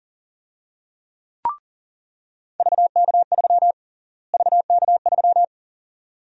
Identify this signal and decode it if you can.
Morse code, heard as a keyed tone, about 40 wpm, 700 Hz. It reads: VK3 VK3
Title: USA Call Sign Prefixes - Repeated 2x - 40wpm